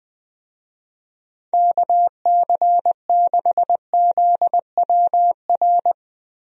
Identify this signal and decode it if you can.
Morse code, heard as a keyed tone, about 20 wpm, 700 Hz. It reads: KC6ZWR
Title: USA Call Signs - 6 Characters - Repeated 3x - 20wpm